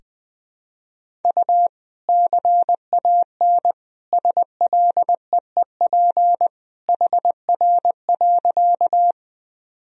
Morse code, heard as a keyed tone, about 21 wpm, 700 Hz. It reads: U CAN SLEEP HR.
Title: Sentences from Top 500 Words and Abbreviations 20wpm